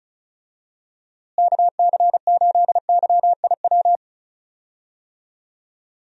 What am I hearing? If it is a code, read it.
Morse code, heard as a keyed tone, about 34 wpm, 700 Hz. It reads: KC8YIW
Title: USA Call Signs - 6 Characters - Repeated 3x - 35wpm